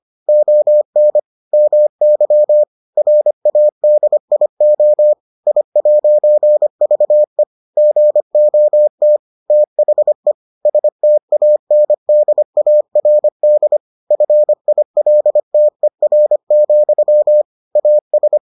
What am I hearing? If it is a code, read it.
Morse code, heard as a keyed tone, about 25 wpm, 600 Hz. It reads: ON MY RADIO I'VE GOT THE STANDARD FILTER, AS